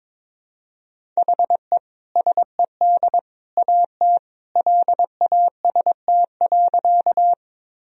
Morse code, heard as a keyed tone, about 21 wpm, 700 Hz. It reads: HE SED AT LAST.
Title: Sentences from Top 300 Words and Abbreviations 22wpm